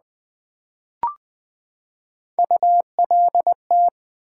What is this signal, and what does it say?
Morse code, heard as a keyed tone, about 20 wpm, 700 Hz. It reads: ULT
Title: Build A Word - 3 Letters At A Time - 8-Letter Words - 20wpm